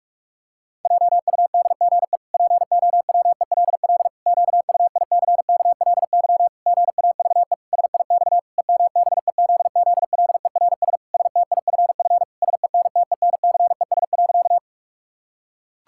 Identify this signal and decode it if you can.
Morse code, heard as a keyed tone, about 48 wpm, 700 Hz. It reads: JUDGE POWELL QUICKLY GAVE SIX EMBEZZLERS STIFF SENTENCES.
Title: Pangram Challenge v2 - 45wpm